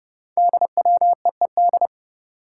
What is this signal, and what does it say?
Morse code, heard as a keyed tone, about 29 wpm, 700 Hz. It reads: DWEED
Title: Code Groups - 5 Letter:Numbers - 30wpm